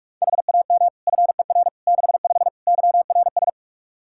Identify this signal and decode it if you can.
Morse code, heard as a keyed tone, about 46 wpm, 700 Hz. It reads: HAM FER 65 YRS
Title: Practice QSO - Exchange 3 - More Info - 45wpm